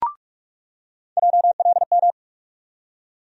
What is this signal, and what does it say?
Morse code, heard as a keyed tone, about 41 wpm, 700 Hz. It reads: JLM
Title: USA Call Sign Postfixes - 3 Letters 45wpm